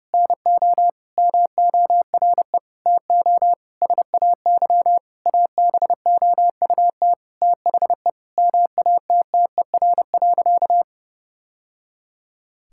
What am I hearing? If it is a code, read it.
Morse code, heard as a keyed tone, about 30 wpm, 700 Hz. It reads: NO MORE TO SAY ABOUT THE MATTER.
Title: Sentences from Top 1000 Words - Review Part 5 - 30wpm